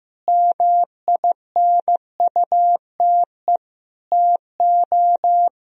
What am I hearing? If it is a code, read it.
Morse code, heard as a keyed tone, about 15 wpm, 700 Hz. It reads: MINUTE TO
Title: Sentences from Top 1000 Words - Review Part 4 - 15wpm